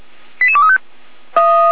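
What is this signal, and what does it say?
Five-tone selective call (ZVEI), heard as quick runs of five tones, tones of about 70 ms. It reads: 88235
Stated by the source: ZVEI five-tone call 88288 received by SDR, recorded on the Feuerwehr Orlamünde (fire brigade) audio server.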